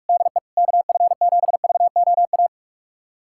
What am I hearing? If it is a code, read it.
Morse code, heard as a keyed tone, about 41 wpm, 690 Hz. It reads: DE KF7VOA
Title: USA Callsign DE Callsign - 6 Characters - 45wpm